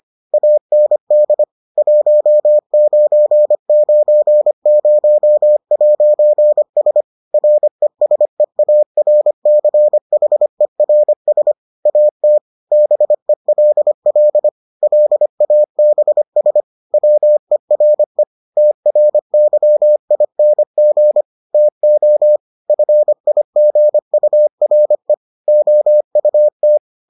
Morse code, heard as a keyed tone, about 25 wpm, 600 Hz. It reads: AND 1990'S RESEARCHERS AT BELL LABS WERE TRYING TO FIGURE OUT